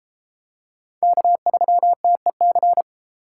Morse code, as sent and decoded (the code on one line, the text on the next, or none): -.- ...-- - . -.-.
K3TEC